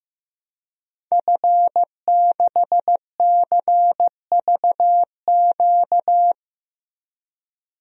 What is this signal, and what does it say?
Morse code, heard as a keyed tone, about 15 wpm, 700 Hz. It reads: F6CVQ
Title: DXCC-100 Call Signs - 5 Characters - Repeated 3x - 15wpm